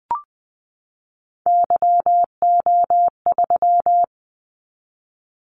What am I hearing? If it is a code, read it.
Morse code, heard as a keyed tone, about 20 wpm, 700 Hz. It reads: YO3
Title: Callsign Prefixes for Top 25 DXCC - 20wpm